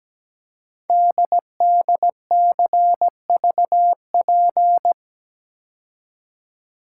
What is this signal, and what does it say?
Morse code, heard as a keyed tone, about 17 wpm, 700 Hz. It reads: DDCVP
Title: Code Groups - 5 Letters - 17wpm